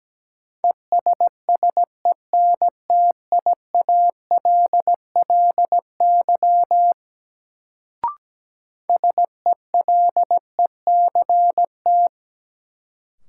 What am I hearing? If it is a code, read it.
Morse code, heard as a keyed tone, about 17 wpm, 700 Hz. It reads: ESSENTIALLY SELECT
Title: Sentences from Top 1300 Words - New Word Drill Alt - 17wpm